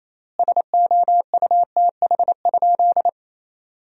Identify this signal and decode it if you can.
Morse code, heard as a keyed tone, about 28 wpm, 700 Hz. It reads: SOUTH?
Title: 4x Word-Spacing - Sentences from Top 400 Words - 28wpm